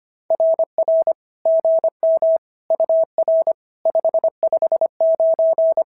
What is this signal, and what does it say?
Morse code, heard as a keyed tone, about 25 wpm, 650 Hz. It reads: RR GM UR 559
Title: SOTA QSO Practice 25wpm